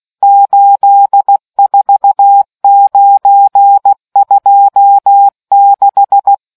Morse code, heard as a keyed tone, about 16 wpm, 800 Hz. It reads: 84926